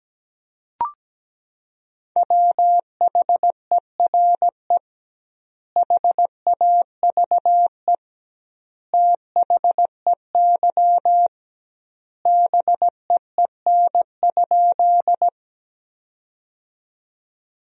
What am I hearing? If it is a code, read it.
Morse code, heard as a keyed tone, about 17 wpm, 700 Hz. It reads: WHERE HAVE THEY BEEN?
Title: 2x Word-Spacing - Sentences from Top 200 Words - 17wpm